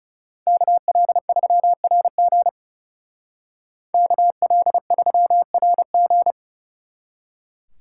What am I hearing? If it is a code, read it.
Morse code, heard as a keyed tone, about 39 wpm, 700 Hz. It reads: KL3RG KL3RG